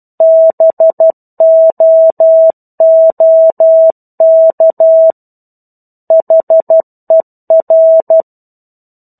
Morse code, heard as a keyed tone, about 12 wpm, 650 Hz. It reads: BOOK HER
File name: 01-Chapter-01 12 WPM